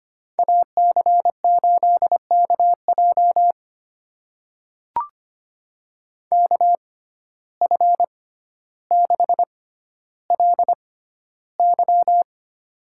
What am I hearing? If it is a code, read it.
Morse code, heard as a keyed tone, about 25 wpm, spaced as for 13 wpm, 700 Hz. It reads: AC8KJ KF6LY
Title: USA Call Signs - 5 Characters - 6x Character Spacing - 25wpm